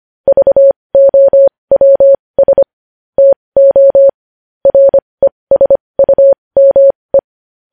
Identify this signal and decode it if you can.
Morse code, heard as a keyed tone, about 25 wpm, spaced as for 20 wpm, 550 Hz. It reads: VOWS TO RESUME
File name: podcast / News headlines-1x-25x20